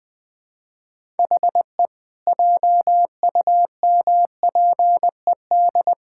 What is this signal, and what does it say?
Morse code, heard as a keyed tone, about 20 wpm, 700 Hz. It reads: HE JUMPED
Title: Sentences with 2-Word Phrases - Spoken Beginning - 20wpm